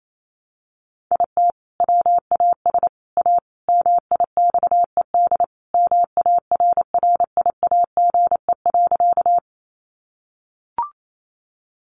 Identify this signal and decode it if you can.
Morse code, heard as a keyed tone, about 28 wpm, 700 Hz. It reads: IT WAS A MIXED MARRIAGE.
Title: Sentences from Top 1500 Words 28wpm